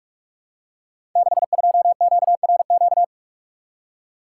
Morse code, heard as a keyed tone, about 44 wpm, 700 Hz. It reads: 62QRQ
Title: Code Groups - 5 Letter:Numbers - 45wpm